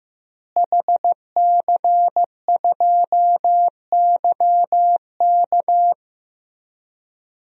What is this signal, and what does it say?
Morse code, heard as a keyed tone, about 15 wpm, 700 Hz. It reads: HC2YK
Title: DXCC-100 Call Signs - 5 Characters - Repeated 2x - 15wpm